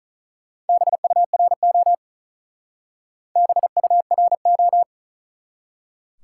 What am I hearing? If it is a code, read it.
Morse code, heard as a keyed tone, about 45 wpm, 700 Hz. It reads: BURO BURO